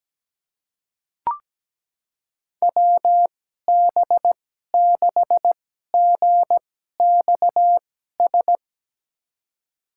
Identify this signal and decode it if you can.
Morse code, heard as a keyed tone, about 17 wpm, spaced as for 12 wpm, 700 Hz. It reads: WB6GXS
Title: USA Call Signs - 6 Characters - 2x Character Spacing - 17wpm